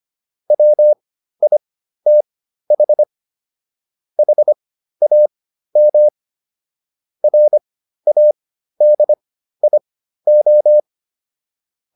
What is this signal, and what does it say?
Morse code, heard as a keyed tone, about 25 wpm, spaced as for 13 wpm, 600 Hz. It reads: WITH HAM RADIO